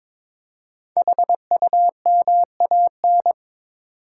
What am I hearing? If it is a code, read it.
Morse code, heard as a keyed tone, about 22 wpm, 700 Hz. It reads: HUMAN
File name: 5 Letter Words - First 3 Letters Given - 22wpm